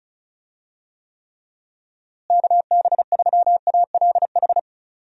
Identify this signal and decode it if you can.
Morse code, heard as a keyed tone, about 34 wpm, 700 Hz. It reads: KB3ALH
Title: USA Call Signs - 6 Characters - Repeated 3x - 35wpm